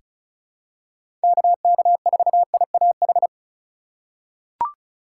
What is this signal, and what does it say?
Morse code, heard as a keyed tone, about 36 wpm, 700 Hz. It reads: KK4IAH
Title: USA Call Signs - 3x Character Spacing - 35wpm